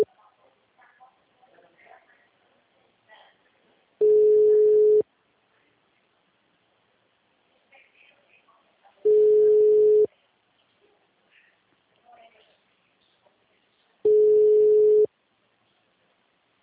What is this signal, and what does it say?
Ringback tone (Europe, 425 Hz).